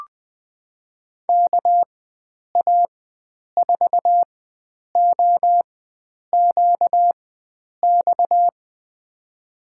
Morse code, heard as a keyed tone, about 20 wpm, spaced as for 9 wpm, 700 Hz. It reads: KA4OQX
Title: USA Call Signs - 6 Characters - 4x Character Spacing - 20wpm